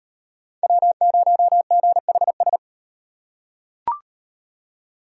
Morse code, heard as a keyed tone, about 38 wpm, 700 Hz. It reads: W0GHS